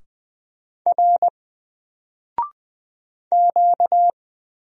Morse code, heard as a keyed tone, about 20 wpm, 700 Hz. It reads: R Q
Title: Single Character (Letter:Number:Punctuation:Prosign) Alt 20wpm